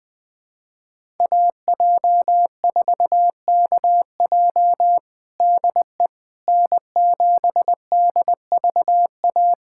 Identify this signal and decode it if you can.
Morse code, heard as a keyed tone, about 20 wpm, 700 Hz. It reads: AJ4KJ DE N7DVA
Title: USA Callsign DE Callsign - 5 Characters - 20wpm